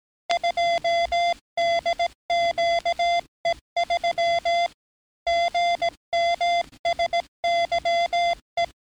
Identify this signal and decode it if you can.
Morse code, heard as a keyed tone, about 17 wpm, 690 Hz. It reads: 2DQE3 GMSYE
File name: training / NAVPERS 11049to11060 / NAVPERS 11057 RA RB